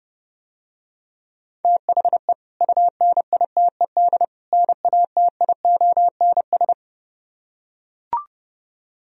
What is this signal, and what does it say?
Morse code, heard as a keyed tone, about 30 wpm, 700 Hz. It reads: THE UNITED NATIONS